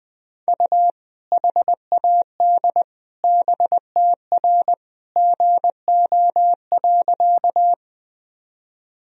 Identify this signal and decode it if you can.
Morse code, heard as a keyed tone, about 20 wpm, 700 Hz. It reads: U HAD BTR GO.
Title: Sentences from Top 400 Words and Abbreviations 20wpm